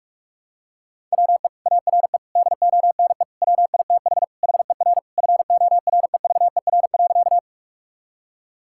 Morse code, heard as a keyed tone, about 46 wpm, 700 Hz. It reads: WE ARE DONE WITH HER FOREVER.
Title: Sentences from Top 1600 Words 45wpm